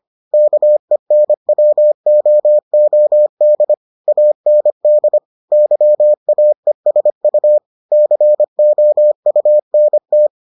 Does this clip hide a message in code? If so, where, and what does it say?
Yes: Morse code from 0.3 to 10.3 s, heard as a keyed tone, about 25 wpm, 600 Hz. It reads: KENWOOD AND YAESU COUNT